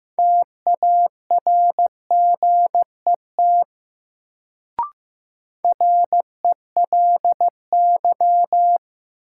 Morse code, heard as a keyed tone, about 15 wpm, 700 Hz. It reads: TARGET RELY
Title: Sentences from Top 1200 Words - New Word Drill - 15wpm